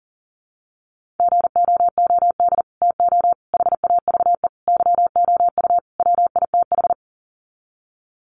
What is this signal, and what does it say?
Morse code, heard as a keyed tone, about 40 wpm, 700 Hz. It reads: GOOD TO HAVE YOU WITH
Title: Sets of 5 Words Encore2 40wpm